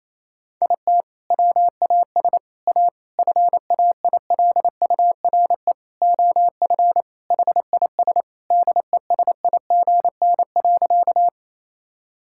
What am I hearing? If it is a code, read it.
Morse code, heard as a keyed tone, about 27 wpm, 700 Hz. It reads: IT WAS A FAILURE OF HIS DESIGN.
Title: Sentences from Top 1000 Words - Review Part 6 - 28wpm